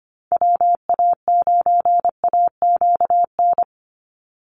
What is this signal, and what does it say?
Morse code, heard as a keyed tone, about 25 wpm, 700 Hz. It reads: WA9AQN
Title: Call Signs - 25wpm